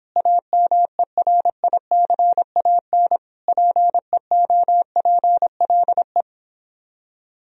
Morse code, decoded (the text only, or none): AMERICAN PEOPLE